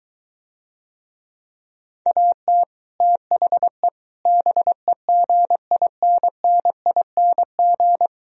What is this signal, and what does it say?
Morse code, heard as a keyed tone, about 23 wpm, 700 Hz. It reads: AT THE BEGINNING